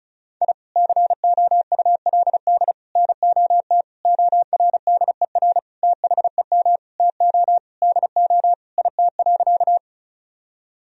Morse code, heard as a keyed tone, about 34 wpm, 700 Hz. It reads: I COULD NOT ORDER THEM TO DO IT.
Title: Sentences from Top 1000 Words - Review Part 2 - 35wpm